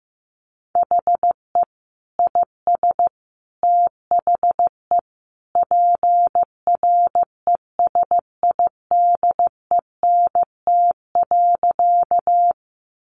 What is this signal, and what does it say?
Morse code, heard as a keyed tone, about 15 wpm, 700 Hz. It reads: HE IS THE PRESIDENT.
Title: Sentences from Top 1000 Words 15wpm